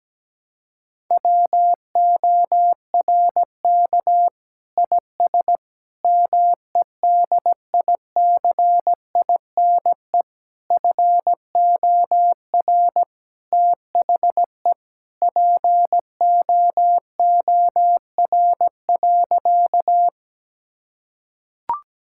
Morse code, heard as a keyed tone, about 17 wpm, 700 Hz. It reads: WORK IS MEDICINE FOR THE POOR.
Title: Sentences from Top 1000 Words - Review Part 6 - 17wpm